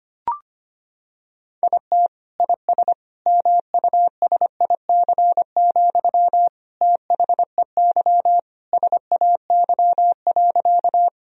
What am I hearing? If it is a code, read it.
Morse code, heard as a keyed tone, about 25 wpm, 700 Hz. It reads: IT IS MUSIC, THEY SAY.
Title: Sentences from Top 1000 Words - Review Part 1 - 25wpm